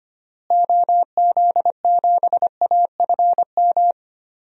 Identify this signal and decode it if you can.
Morse code, heard as a keyed tone, about 25 wpm, 690 Hz. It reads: OZ7AFM
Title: DXCC-100 Call Signs - 6 Characters - Repeated 3x - 25wpm